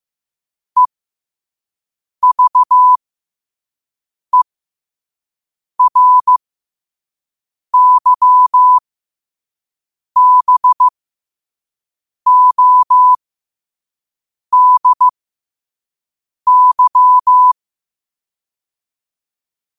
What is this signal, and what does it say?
Morse code, heard as a keyed tone, about 15 wpm, spaced as for 5 wpm, 1000 Hz. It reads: EVERYBODY
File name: qotdfiles / 5wpm